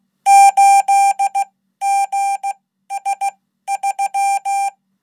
Morse code, heard as a keyed tone, about 16 wpm, spaced as for 12 wpm, 790 Hz. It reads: 8GS3
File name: Code